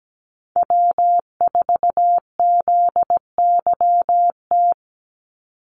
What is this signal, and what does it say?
Morse code, heard as a keyed tone, about 17 wpm, 700 Hz. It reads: W4ZYT
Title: Call Signs Repeated 3x - 17wpm